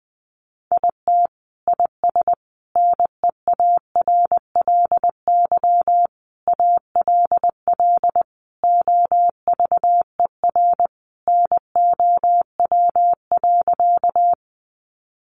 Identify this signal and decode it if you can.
Morse code, heard as a keyed tone, about 20 wpm, 700 Hz. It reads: IT IS NEARLY ALL OVER NOW.
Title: Sentences from Top 1300 Words 20wpm